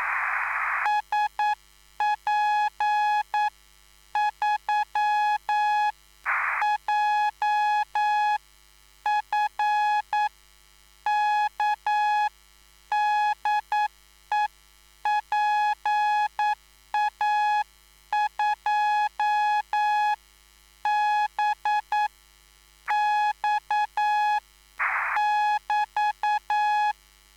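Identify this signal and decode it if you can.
Morse code, heard as a keyed tone, about 9 wpm, spaced as for 7 wpm, 840 Hz. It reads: SP3JFK DE PA2BX <BT>
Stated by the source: Live CW QSO between PA2BX and SP3JFK on the 14055 kHz QRS frequency, at a basic 8 wpm.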